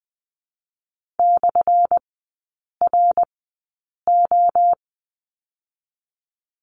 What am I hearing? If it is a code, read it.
Morse code, heard as a keyed tone, about 20 wpm, spaced as for 8 wpm, 700 Hz. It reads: /RO
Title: 184 Lesson 30e - Letter b - Three Characters 20wpm